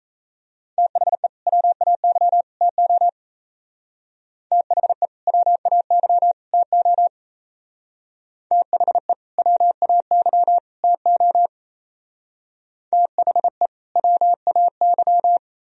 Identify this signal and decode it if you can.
Morse code, heard as a keyed tone, about 35 wpm, 700 Hz. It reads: THE WAY TO THE WAY TO THE WAY TO THE WAY